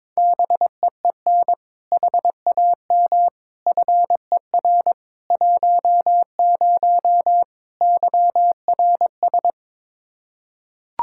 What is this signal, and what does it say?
Morse code, heard as a keyed tone, about 22 wpm, 700 Hz. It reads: BEEN HAM FER 10 YRS
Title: Practice QSO - Exchange 3 - More Info - 22wpm